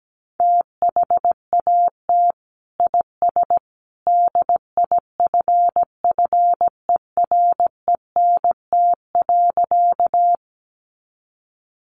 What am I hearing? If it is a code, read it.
Morse code, heard as a keyed tone, about 17 wpm, 700 Hz. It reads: THAT IS DIFFERENT.
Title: Sentences from Top 600 Words 17wpm